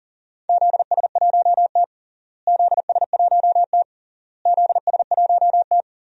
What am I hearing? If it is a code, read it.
Morse code, heard as a keyed tone, about 40 wpm, 700 Hz. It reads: ZS1T ZS1T ZS1T
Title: DXCC-100 Call Signs - 4 Characters - Repeated 3x - 40wpm